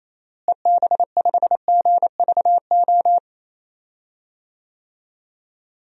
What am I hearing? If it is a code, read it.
Morse code, heard as a keyed tone, about 27 wpm, 700 Hz. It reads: EB5GVO
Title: DXCC-100 Call Signs - 6 Characters - Repeated 3x - 28wpm